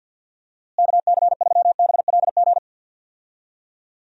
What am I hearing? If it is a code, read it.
Morse code, heard as a keyed tone, about 53 wpm, 700 Hz. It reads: KC3BLG
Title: USA Call Signs - 6 Characters - Repeated 3x - 50wpm